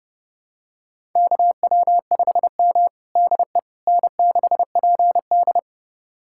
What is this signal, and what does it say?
Morse code, heard as a keyed tone, about 32 wpm, 700 Hz. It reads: KW5M DE N6PD
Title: USA Callsign DE Callsign - 4 Characters - 30wpm